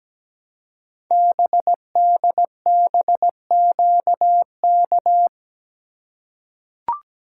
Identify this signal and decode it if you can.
Morse code, heard as a keyed tone, about 17 wpm, 700 Hz. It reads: BDBQK
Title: Code Groups - 5 Letters - 17wpm